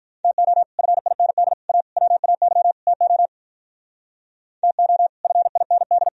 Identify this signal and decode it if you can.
Morse code, heard as a keyed tone, about 45 wpm, 700 Hz. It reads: TO FIND A WAY TO TO FIND